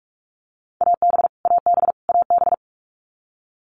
Morse code, heard as a keyed tone, about 44 wpm, 700 Hz. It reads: AB AB AB
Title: Canadian Provinces and Territories - Repeated 3x - 45wpm